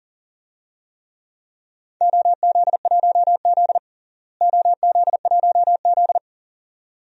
Morse code, heard as a keyed tone, about 37 wpm, 700 Hz. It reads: OZ1Z OZ1Z
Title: DXCC-100 Call Signs - 4 Characters - Repeated 2x - 40wpm